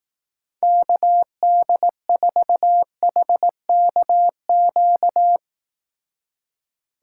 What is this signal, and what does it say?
Morse code, heard as a keyed tone, about 18 wpm, 700 Hz. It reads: KD4HKQ
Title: Speed Racing - USA 2x3 Call Signs 15wpm